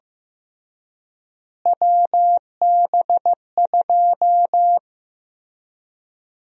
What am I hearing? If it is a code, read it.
Morse code, heard as a keyed tone, about 15 wpm, 700 Hz. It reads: WB2
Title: USA Call Sign Prefixes - Repeated 2x - 15wpm